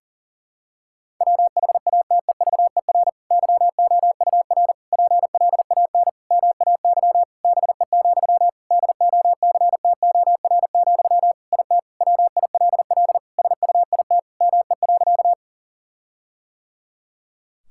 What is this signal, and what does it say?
Morse code, heard as a keyed tone, about 37 wpm, 700 Hz. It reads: WHATEVER YOUR PLAN MAY BE, DOCTOR, IT WILL SUIT ME.
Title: Sentences from Top 1600 Words 40wpm